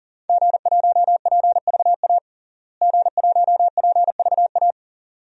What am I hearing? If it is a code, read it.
Morse code, heard as a keyed tone, about 40 wpm, 700 Hz. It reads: G1PVA G1PVA
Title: DXCC-100 Call Signs - 5 Characters - Repeated 2x - 40wpm